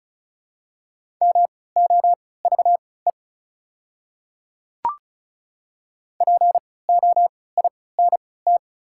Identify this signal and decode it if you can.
Morse code, heard as a keyed tone, about 34 wpm, spaced as for 20 wpm, 700 Hz. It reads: MOVE POINT
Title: 3x Character Spacing - Top 500 Words - 35wpm